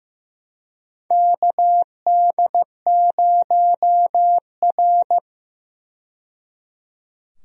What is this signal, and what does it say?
Morse code, heard as a keyed tone, about 15 wpm, 700 Hz. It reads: KD0R
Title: USA Call Signs - 4 Characters - 15wpm